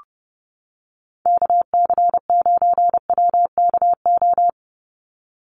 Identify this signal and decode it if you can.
Morse code, heard as a keyed tone, about 30 wpm, 700 Hz. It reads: KC9WKO
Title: USA Call Signs 30wpm